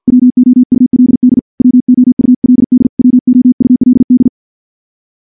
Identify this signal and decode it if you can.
Morse code, heard as a keyed tone, about 49 wpm, 250 Hz. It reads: WOULD WOULD WOULD
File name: morse / morse-training-50WPM